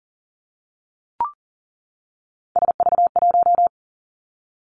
Morse code, heard as a keyed tone, about 38 wpm, 700 Hz. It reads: SV1
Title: Callsign Prefixes for Top 50 DXCC - 40wpm